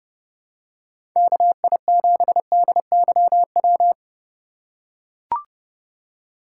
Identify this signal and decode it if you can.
Morse code, heard as a keyed tone, about 30 wpm, 700 Hz. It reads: KI7DYW
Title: USA Call Signs - 6 Characters - 2x Character Spacing - 30wpm